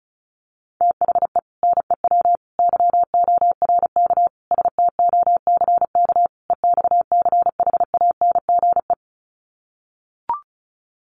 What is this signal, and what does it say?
Morse code, heard as a keyed tone, about 34 wpm, 700 Hz. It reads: THE NEW YORK STOCK EXCHANGE
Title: Sets of 5 Words 35wpm